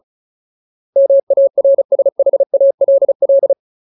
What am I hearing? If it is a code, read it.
Morse code, heard as a keyed tone, about 37 wpm, 550 Hz. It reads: MARSHALL
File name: Ultimate Word List / 35WPM Ultimate